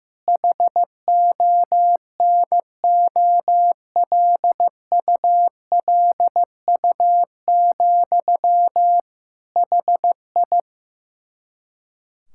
Morse code, heard as a keyed tone, about 15 wpm, 700 Hz. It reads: HONOLULU, HI
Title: Practice QSO - Exchange 1 - Beginning - 15wpm